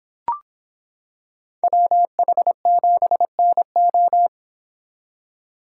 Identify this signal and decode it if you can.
Morse code, heard as a keyed tone, about 25 wpm, 700 Hz. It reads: WH7NO